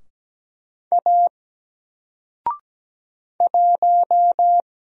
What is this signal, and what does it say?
Morse code, heard as a keyed tone, about 17 wpm, 700 Hz. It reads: A 1
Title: Single Letter-Number 17wpm